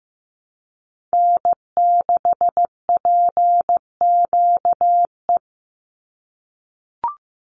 Morse code, heard as a keyed tone, about 15 wpm, 700 Hz. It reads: N6PQE
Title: USA Call Signs 15wpm